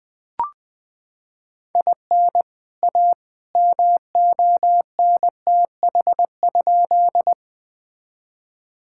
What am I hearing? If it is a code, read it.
Morse code, heard as a keyed tone, about 20 wpm, 700 Hz. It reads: IN A MONTH?
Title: Sentences from Top 1000 Words - Review Part 5 - 20wpm